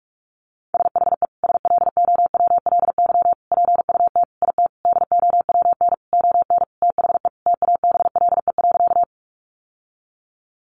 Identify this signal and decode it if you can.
Morse code, heard as a keyed tone, about 44 wpm, 700 Hz. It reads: SHE SLOWLY PUT IT DOWN ON THE TABLE.
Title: Sentences from Top 1100 Words 45wpm